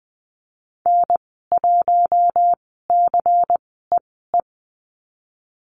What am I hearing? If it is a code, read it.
Morse code, heard as a keyed tone, about 20 wpm, spaced as for 15 wpm, 700 Hz. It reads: N1CEE
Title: 186 Lesson 30g - Letter b - Call Signs 20wpm